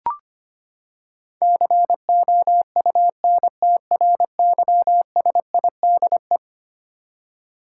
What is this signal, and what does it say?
Morse code, heard as a keyed tone, about 25 wpm, 700 Hz. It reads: COUNTRYSIDE
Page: morse-code-ninja-practice-25wpm